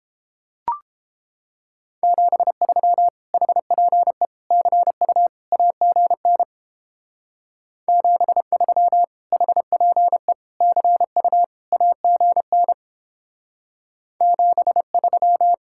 Morse code, heard as a keyed tone, about 29 wpm, 700 Hz. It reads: 73 HPE CU AGN 73 HPE CU AGN 73